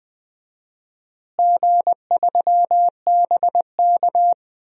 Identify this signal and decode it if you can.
Morse code, heard as a keyed tone, about 20 wpm, 690 Hz. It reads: G3BK
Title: DXCC-100 Call Signs - 4 Characters - Repeated 3x - 20wpm